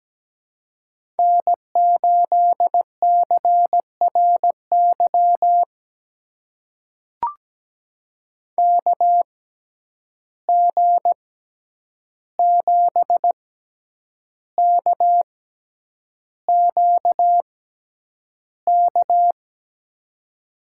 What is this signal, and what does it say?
Morse code, heard as a keyed tone, about 17 wpm, spaced as for 8 wpm, 700 Hz. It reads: N8CRY KG7KQK
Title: USA Call Signs - 6x Character Spacing - 17wpm